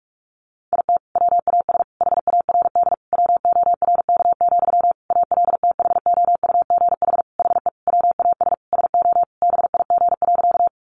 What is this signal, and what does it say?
Morse code, heard as a keyed tone, about 44 wpm, 700 Hz. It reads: IT WAS HARD WORK, ALTHOUGH HE WAS SO BIG.
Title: Sentences from Top 600 Words 45wpm